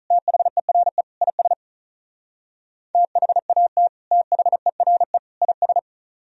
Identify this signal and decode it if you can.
Morse code, heard as a keyed tone, about 44 wpm, 700 Hz. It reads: THERE IS THAT THERE IS
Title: Speed-Racing - Sets of 3 Words 35wpm